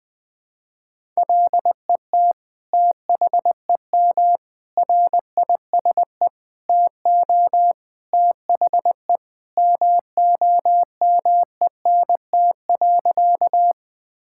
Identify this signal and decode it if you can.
Morse code, heard as a keyed tone, about 20 wpm, 700 Hz. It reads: LET THEM RISE TO THE MOMENT.
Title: Sentences from Top 1000 Words - Review Part 6 - 20wpm